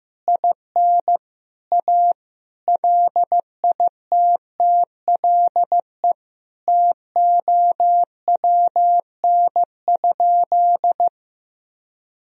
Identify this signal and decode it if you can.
Morse code, heard as a keyed tone, about 15 wpm, 700 Hz. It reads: IN A LITTLE TOWN?
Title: Sentences from Top 1000 Words - Review Part 2 - 15wpm